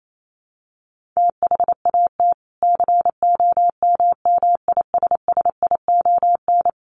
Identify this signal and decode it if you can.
Morse code, heard as a keyed tone, about 28 wpm, 700 Hz. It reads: THAT COMMISSION
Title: Sentences from Top 1400 Words 28wpm